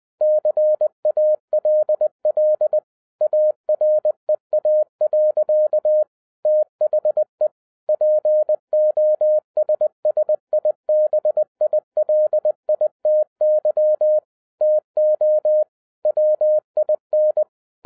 Morse code, heard as a keyed tone, about 20 wpm, 610 Hz. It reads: CALL AREA. THE POSSIBILITY TO WIN
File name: 2012-01-mp3 / n15-INVESTIGATION INTO THE POSSIBILITY OF WINNING PREFIX VERSION 3 AWARD-20wpm